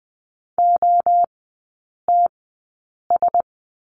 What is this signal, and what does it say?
Morse code, heard as a keyed tone, about 20 wpm, spaced as for 8 wpm, 700 Hz. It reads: OTS
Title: 133 Lesson 22e - Question Mark - Three Characters 20wpm